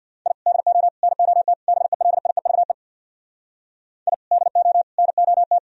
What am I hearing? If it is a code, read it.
Morse code, heard as a keyed tone, about 64 wpm, 700 Hz. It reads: I DO NOT BELIEVE I DO NOT